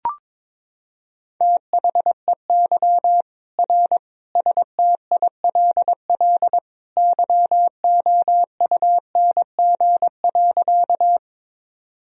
Morse code, heard as a keyed tone, about 22 wpm, 700 Hz. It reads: THEY R STILL YOUNG.
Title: Sentences from Top 400 Words and Abbreviations 22wpm